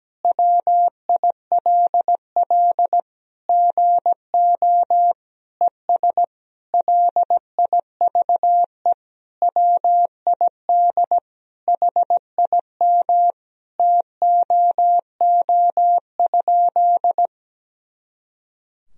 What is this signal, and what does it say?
Morse code, heard as a keyed tone, about 17 wpm, 700 Hz. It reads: WILL GO ES LIVE WID HIM TOO?
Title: Sentences from Top 500 Words and Abbreviations 17wpm